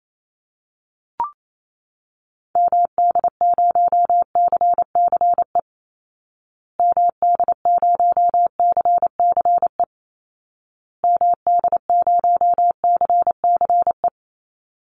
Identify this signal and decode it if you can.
Morse code, heard as a keyed tone, about 28 wpm, 700 Hz. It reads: MD0CCE MD0CCE MD0CCE
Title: Call Signs Repeated 2x - 28wpm